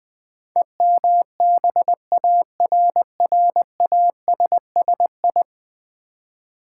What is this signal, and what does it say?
Morse code, heard as a keyed tone, about 20 wpm, 700 Hz. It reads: EMBARRASSI
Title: Build A Word - 5 Letters At A Time - 12-Letter Words - 20wpm